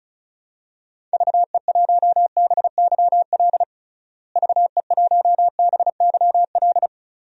Morse code, heard as a keyed tone, about 36 wpm, 700 Hz. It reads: VE1BYL VE1BYL
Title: DXCC-100 Call Signs - 6 Characters - Repeated 3x - 35wpm